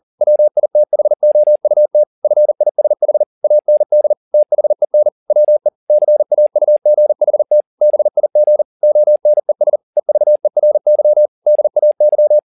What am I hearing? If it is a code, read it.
Morse code, heard as a keyed tone, about 40 wpm, 600 Hz. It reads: WITHOUT FISH AND THEN WE CAUGHT BIG ONES EVERY DAY
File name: the old man and the sea 1 40wpm